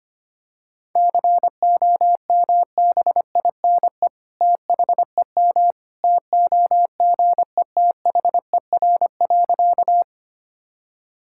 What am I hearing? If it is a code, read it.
Morse code, heard as a keyed tone, about 25 wpm, 700 Hz. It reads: COMBINE THEM TOGETHER.
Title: Sentences from Top 1600 Words 25wpm